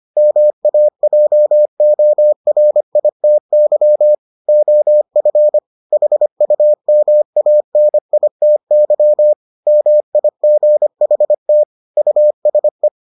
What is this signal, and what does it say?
Morse code, heard as a keyed tone, about 25 wpm, 600 Hz. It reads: MAJORITY OF HUMANITY MIGHT USE